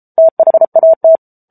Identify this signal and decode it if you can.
Morse code, heard as a keyed tone, about 34 wpm, 650 Hz. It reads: THAT
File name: Book Two / 07-Chapter 7 - 33 WPM